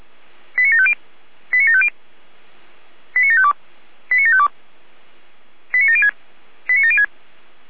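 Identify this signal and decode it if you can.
Five-tone selective call (ZVEI), heard as quick runs of five tones, tones of about 70 ms. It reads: 89750 89750 89752 89752 89706 89706
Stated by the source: ZVEI five-tone selective call received on an SDR by Feuerwehr Orlamünde.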